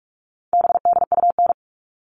Morse code, heard as a keyed tone, about 44 wpm, 700 Hz. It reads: BDUN
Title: All Cut Numbers - 4-Digit 45wpm